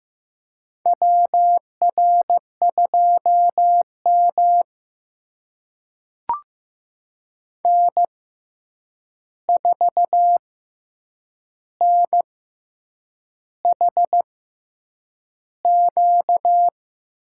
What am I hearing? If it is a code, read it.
Morse code, heard as a keyed tone, about 15 wpm, spaced as for 7 wpm, 700 Hz. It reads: WR2M N4NHQ
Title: USA Call Signs - 6x Character Spacing - 15wpm